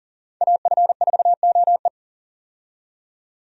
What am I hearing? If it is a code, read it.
Morse code, heard as a keyed tone, about 40 wpm, 700 Hz. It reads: AF4OE
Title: USA Call Signs - 5 Characters - 40wpm